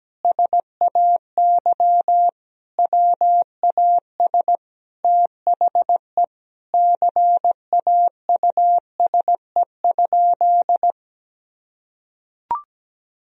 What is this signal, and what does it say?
Morse code, heard as a keyed tone, about 17 wpm, 700 Hz. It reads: SAY WAS THE CAUSE?
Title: Sentences from Top 1000 Words - Review Part 1 - 17wpm